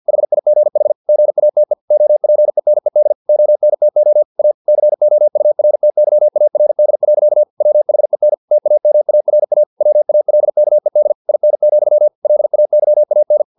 Morse code, heard as a keyed tone, about 50 wpm, 600 Hz. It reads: HIGH GATE OPENED ONTO A COURTYARD. WHEN TAMARA WALKED IN, LAYAN
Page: Never 49wpm